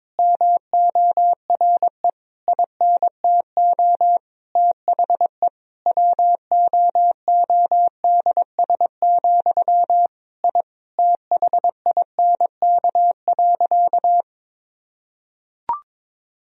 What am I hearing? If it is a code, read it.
Morse code, heard as a keyed tone, about 21 wpm, 700 Hz. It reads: MORE INTO THE WOODS, I THINK.
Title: Sentences from Top 1000 Words - Review Part 2 - 22wpm